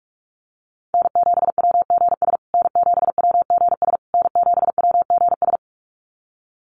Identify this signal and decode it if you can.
Morse code, heard as a keyed tone, about 44 wpm, 700 Hz. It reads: N7WGS N7WGS N7WGS
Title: USA Call Signs Repeated 3x 45wpm